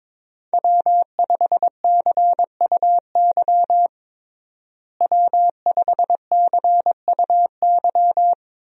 Morse code, heard as a keyed tone, about 22 wpm, 700 Hz. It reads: W5CUY W5CUY
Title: USA Call Signs - 5 Characters - Repeated 2x - 22wpm